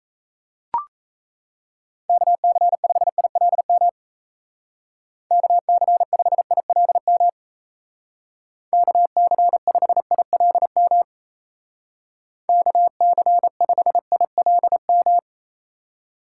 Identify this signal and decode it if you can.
Morse code, heard as a keyed tone, about 35 wpm, 700 Hz. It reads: KC5ILM KC5ILM KC5ILM KC5ILM
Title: Speed Racing - USA 2x3 Call Signs 28wpm